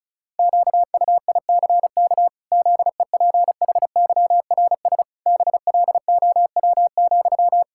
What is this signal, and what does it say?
Morse code, heard as a keyed tone, about 34 wpm, 700 Hz. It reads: QUICK ZEPHYRS BLOW,
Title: Pangram Challenge v2 - 35wpm